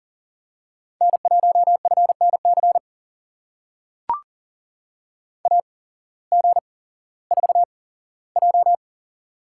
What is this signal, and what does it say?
Morse code, heard as a keyed tone, about 42 wpm, spaced as for 19 wpm, 700 Hz. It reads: N1FNC AG4J
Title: USA Call Signs - 5 Characters - 8x Character Spacing - 40wpm